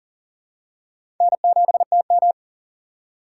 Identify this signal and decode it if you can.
Morse code, heard as a keyed tone, about 40 wpm, 700 Hz. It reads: N7TM
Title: USA Call Signs - 4 Characters - 4x Character Spacing - 40wpm